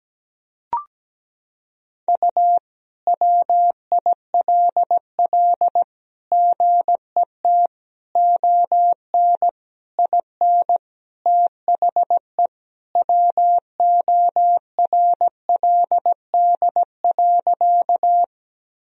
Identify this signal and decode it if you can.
Morse code, heard as a keyed tone, about 17 wpm, 700 Hz. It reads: U WILL GET ON IN THE WORLD.
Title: Sentences from Top 200 Words and Abbreviations 17wpm